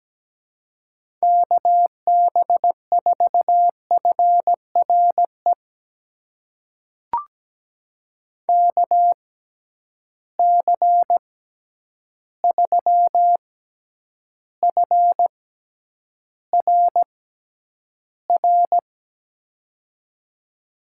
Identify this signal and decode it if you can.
Morse code, heard as a keyed tone, about 17 wpm, spaced as for 9 wpm, 700 Hz. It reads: KB4FRE KC3FRR
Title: USA Call Signs - 6 Characters - 6x Character Spacing - 17wpm